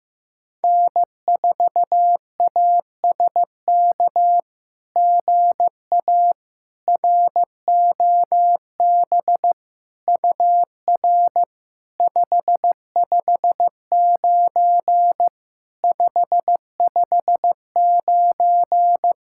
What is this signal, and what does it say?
Morse code, heard as a keyed tone, about 15 wpm, 700 Hz. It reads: N4ASK GA ROB UR 559 559
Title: SOTA QSO Practice 15wpm